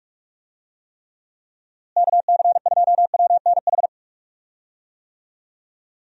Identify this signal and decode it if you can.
Morse code, heard as a keyed tone, about 41 wpm, 700 Hz. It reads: KC2WNH
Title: USA Call Signs - 6 Characters - Repeated 3x - 45wpm